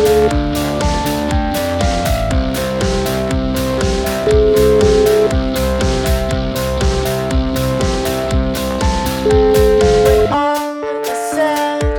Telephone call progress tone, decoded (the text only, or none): ringback tone